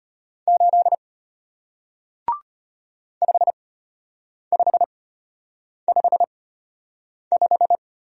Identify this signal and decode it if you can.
Morse code, heard as a keyed tone, about 34 wpm, spaced as for 8 wpm, 700 Hz. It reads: 8 5555